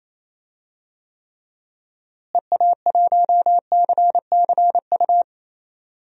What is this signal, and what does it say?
Morse code, heard as a keyed tone, about 28 wpm, 700 Hz. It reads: EA1CCU
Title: DXCC-100 Call Signs - 6 Characters - Repeated 3x - 28wpm